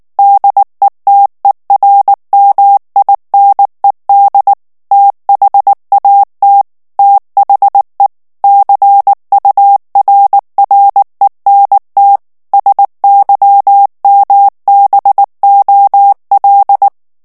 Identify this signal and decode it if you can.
Morse code, heard as a keyed tone, about 19 wpm, 800 Hz. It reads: DETERMINED THAT THE CURRENT SYMBOL
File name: ANGLAIS texte clair 3 vitesse 18 mots minute